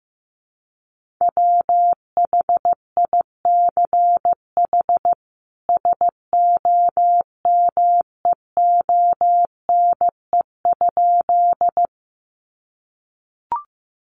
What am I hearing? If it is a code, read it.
Morse code, heard as a keyed tone, about 15 wpm, 700 Hz. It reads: WHICH SOMEONE?
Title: Sentences from Top 600 Words 15wpm